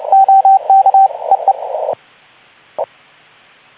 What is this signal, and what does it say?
Morse code, heard as a keyed tone, about 29 wpm, 780 Hz, with radio noise. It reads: OK EE
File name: Archiv / CUC189 / ok2zcq cuc 189